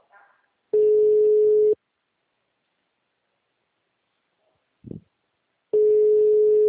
Ringback tone (Europe, 425 Hz).
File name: out- -2015- - - .5283